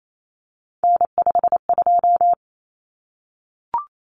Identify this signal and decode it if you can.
Morse code, heard as a keyed tone, about 28 wpm, 700 Hz. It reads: N52
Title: Common Cut Numbers - 3-Digit 28wpm